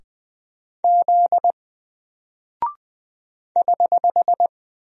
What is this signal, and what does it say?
Morse code, heard as a keyed tone, about 20 wpm, 700 Hz. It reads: Z <HH>
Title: Single Character (Letter:Number:Punctuation:Prosign) 20wpm